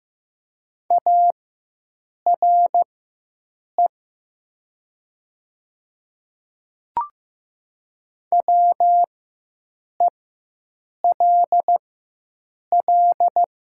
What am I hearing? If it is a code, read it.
Morse code, heard as a keyed tone, about 15 wpm, spaced as for 7 wpm, 700 Hz. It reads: ARE WELL
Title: 4x Character Spacing - Top 200 Words - 15wpm